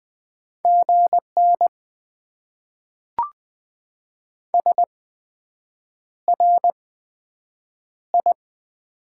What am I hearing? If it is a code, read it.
Morse code, heard as a keyed tone, about 20 wpm, spaced as for 7 wpm, 700 Hz. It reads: GN SRI
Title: Top 90 QSO Elements - 8x Character Spacing - 20wpm